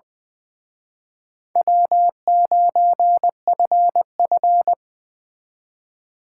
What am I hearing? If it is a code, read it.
Morse code, heard as a keyed tone, about 21 wpm, 700 Hz. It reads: W9FF
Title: Speed Racing - USA Call Signs 20wpm